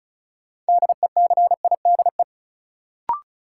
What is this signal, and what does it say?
Morse code, heard as a keyed tone, about 36 wpm, 700 Hz. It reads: DECIDE
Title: Sentences from Top 500 Words - New Word Drill - 35wpm